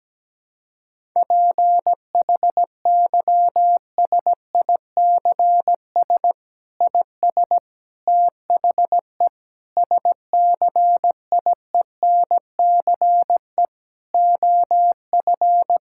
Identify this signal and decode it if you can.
Morse code, heard as a keyed tone, about 17 wpm, 700 Hz. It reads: PHYSICS IS THE SCIENCE OF
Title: Sentences from Top 1000 Words - Review Part 5 - 17wpm